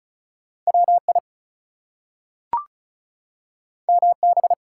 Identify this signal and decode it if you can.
Morse code, heard as a keyed tone, about 36 wpm, 700 Hz. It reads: WI MB
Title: US States and Canadian Provinces 35wpm